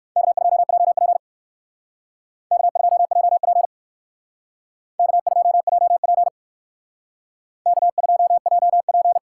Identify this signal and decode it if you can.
Morse code, heard as a keyed tone, about 54 wpm, 700 Hz. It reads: K2JP K2JP K2JP K2JP
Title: Speed Racing - USA 1x2 Call Signs 45wpm